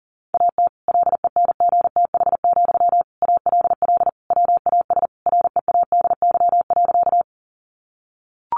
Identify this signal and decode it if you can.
Morse code, heard as a keyed tone, about 40 wpm, 700 Hz. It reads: AT LENGTH, ALL WAS READY.
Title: Sentences from Top 1000 Words 40wpm